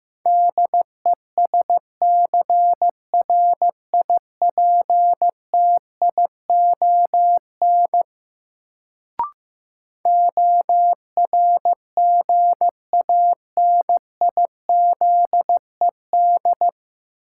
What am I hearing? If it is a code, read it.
Morse code, heard as a keyed tone, about 15 wpm, 700 Hz. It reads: DESCRIPTION ORGANIZED
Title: Sentences from Top 1300 Words - New Word Drill - 15wpm